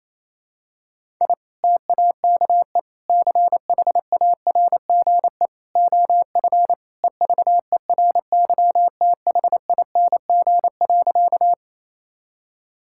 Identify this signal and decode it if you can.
Morse code, heard as a keyed tone, about 28 wpm, 700 Hz. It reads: I TAKE CHARGE OF EVERYTHING.
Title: Sentences from Top 1000 Words - Review Part 6 - 28wpm